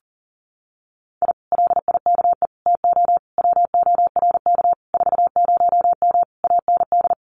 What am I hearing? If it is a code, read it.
Morse code, heard as a keyed tone, about 37 wpm, 700 Hz. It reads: I LIKE TO WORK 40M AND
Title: Common QSO Phrases Encore 40wpm